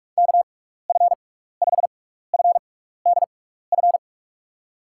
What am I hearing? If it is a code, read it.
Morse code, heard as a keyed tone, about 46 wpm, spaced as for 16 wpm, 700 Hz. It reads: KF5FDF
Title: USA Call Signs - 6 Characters - 6x Character Spacing - 45wpm